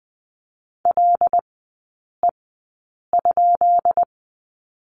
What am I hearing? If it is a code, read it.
Morse code, heard as a keyed tone, about 20 wpm, spaced as for 8 wpm, 700 Hz. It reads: LE?
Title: 133 Lesson 22e - Question Mark - Three Characters 20wpm